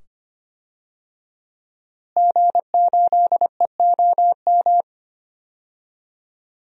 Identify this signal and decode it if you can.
Morse code, heard as a keyed tone, about 25 wpm, 700 Hz. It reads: G8EOM
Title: DXCC-100 Call Signs - 5 Characters - Repeated 2x - 25wpm